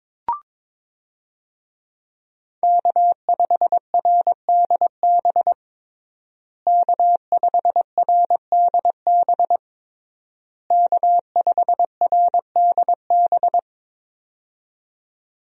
Morse code, heard as a keyed tone, about 22 wpm, 700 Hz. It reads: K5RDB K5RDB K5RDB
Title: USA Call Signs - 5 Characters - Repeated 3x - 22wpm